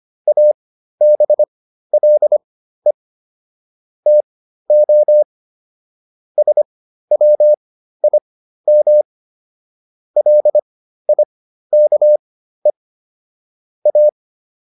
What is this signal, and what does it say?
Morse code, heard as a keyed tone, about 25 wpm, spaced as for 13 wpm, 600 Hz. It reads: ABLE TO SWIM LIKE A